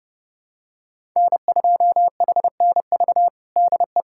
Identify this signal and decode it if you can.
Morse code, heard as a keyed tone, about 32 wpm, 700 Hz. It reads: N2HNV DE
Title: USA Callsign DE Callsign - 5 Characters - 30wpm